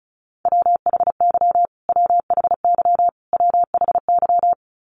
Morse code, heard as a keyed tone, about 35 wpm, 700 Hz. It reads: WHY WHY WHY
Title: Top 200 Words Repeated 35wpm